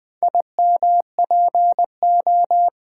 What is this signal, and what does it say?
Morse code, heard as a keyed tone, about 20 wpm, 700 Hz. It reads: IMPO
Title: Build A Word - 2 Letters At A Time - 6-Letter Words - 20wpm